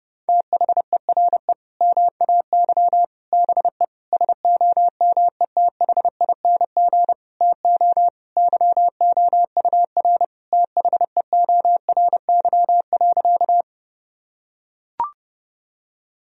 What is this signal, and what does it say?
Morse code, heard as a keyed tone, about 30 wpm, 700 Hz. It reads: THERE MAY BE SOMETHING TO YOUR THEORY.
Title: Sentences from Top 1000 Words - Review Part 3 - 30wpm